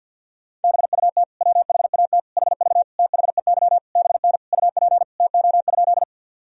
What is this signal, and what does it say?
Morse code, heard as a keyed tone, about 52 wpm, 700 Hz. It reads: BUT WHAT HV THEY BN UP TO?
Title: Sentences from Top 100 Words and Abbreviations 50wpm